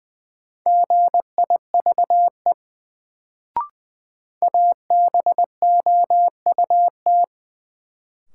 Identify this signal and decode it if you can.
Morse code, heard as a keyed tone, about 20 wpm, 700 Hz. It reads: GIVE ABOUT
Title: Sets of 2 Words - New Word Drill - 20wpm